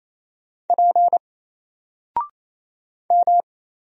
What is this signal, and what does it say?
Morse code, heard as a keyed tone, about 28 wpm, 700 Hz. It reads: P M
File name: Single-Letter-Number-28wpm